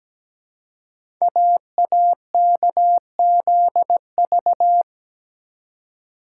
Morse code, heard as a keyed tone, about 17 wpm, 700 Hz. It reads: AAKZV
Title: Code Groups - 5 Letter:Numbers - 17wpm